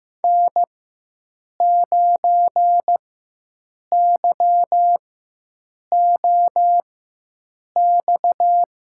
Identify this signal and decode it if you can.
Morse code, heard as a keyed tone, about 15 wpm, spaced as for 7 wpm, 700 Hz. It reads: N9YOX
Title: USA Call Signs - 5 Characters - 4x Character Spacing - 15wpm